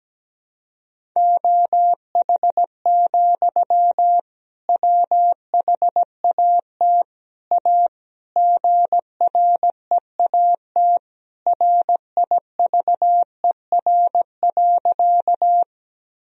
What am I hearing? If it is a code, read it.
Morse code, heard as a keyed tone, about 17 wpm, 700 Hz. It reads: OH, WHAT A GREAT RIVER.
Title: Sentences from Top 1000 Words - Review Part 1 - 17wpm